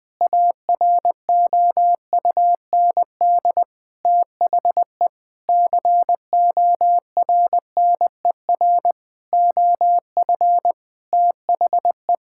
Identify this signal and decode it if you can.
Morse code, heard as a keyed tone, about 20 wpm, 700 Hz. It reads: AROUND THE CORNER OF THE
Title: Sentences from Top 1600 Words 20wpm